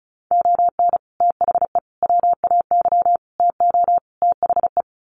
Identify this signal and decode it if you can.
Morse code, heard as a keyed tone, about 34 wpm, 700 Hz. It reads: ON THE WAY TO THE
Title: Sets of 5 Words 35wpm